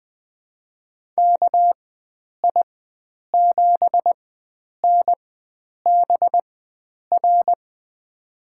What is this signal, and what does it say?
Morse code, heard as a keyed tone, about 20 wpm, spaced as for 9 wpm, 700 Hz. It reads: KI7NBR
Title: USA Call Signs - 6 Characters - 4x Character Spacing - 20wpm